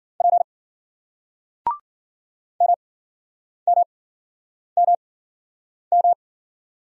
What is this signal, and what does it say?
Morse code, heard as a keyed tone, about 53 wpm, spaced as for 9 wpm, 700 Hz. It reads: P MMMM